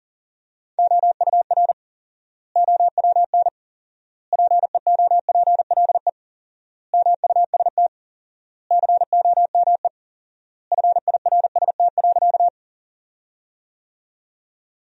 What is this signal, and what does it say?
Morse code, heard as a keyed tone, about 40 wpm, 700 Hz. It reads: OUR OWN PEOPLE MUST COME FIRST.
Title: 4x Word-Spacing - Sentences from Top 300 Words - 40wpm